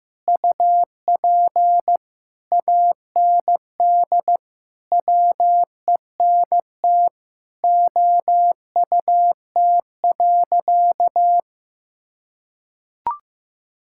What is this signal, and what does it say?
Morse code, heard as a keyed tone, about 15 wpm, 700 Hz. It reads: UP AND WENT OUT.
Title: Sentences from Top 1600 Words 15wpm